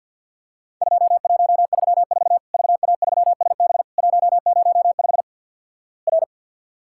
Morse code, heard as a keyed tone, about 48 wpm, 700 Hz. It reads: 2134 VA3SB 105 R
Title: Worked All Europe DX Contest - QTC Exchange - 50wpm